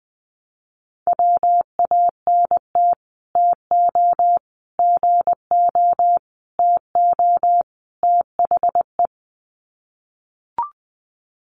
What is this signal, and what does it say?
Morse code, heard as a keyed tone, about 20 wpm, 700 Hz. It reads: WANT TO GO TO THE
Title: Sets of 5 Words Encore2 20wpm